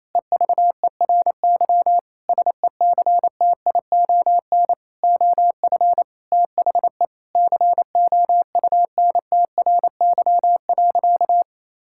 Morse code, heard as a keyed tone, about 28 wpm, 700 Hz. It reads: EVERY SECTION OF THE COUNTRY.
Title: Sentences from Top 1000 Words - Review Part 5 - 28wpm